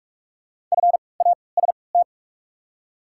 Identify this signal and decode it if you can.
Morse code, heard as a keyed tone, about 47 wpm, 690 Hz. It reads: F A S T
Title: 3x Character Spacing - Top 500 Words - 45wpm